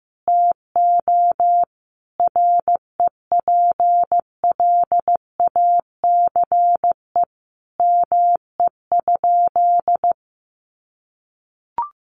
Morse code, heard as a keyed tone, about 15 wpm, 700 Hz. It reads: TO REPLACE ME?
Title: Sentences from Top 1400 Words 15wpm